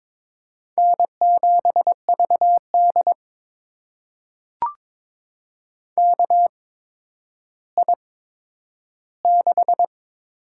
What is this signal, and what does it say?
Morse code, heard as a keyed tone, about 22 wpm, spaced as for 11 wpm, 700 Hz. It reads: N7VD KI6
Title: USA Call Signs - 8x Character Spacing - 22wpm